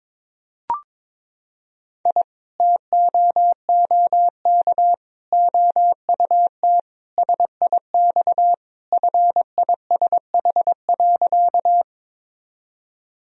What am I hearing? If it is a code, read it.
Morse code, heard as a keyed tone, about 22 wpm, 700 Hz. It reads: I TOOK OUT SIX FISH.
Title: Sentences from Top 1000 Words - Review Part 2 - 22wpm